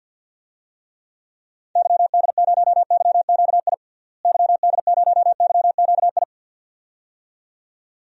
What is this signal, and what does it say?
Morse code, heard as a keyed tone, about 50 wpm, 700 Hz. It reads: YD0YQI YD0YQI
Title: DXCC-100 Call Signs - 6 Characters - Repeated 2x - 50wpm